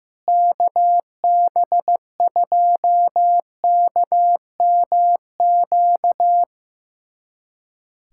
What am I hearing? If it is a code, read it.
Morse code, heard as a keyed tone, about 15 wpm, 700 Hz. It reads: KB2KMQ
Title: USA Callsign DE Callsign - 6 Characters - 15wpm